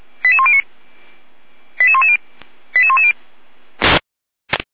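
Five-tone selective call (ZVEI), heard as quick runs of five tones, tones of about 70 ms. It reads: 88109 88110 88110
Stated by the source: ZVEI five-tone call received by SDR, from audio.f85360.feuerwehr-orlamuende.org fire brigade server.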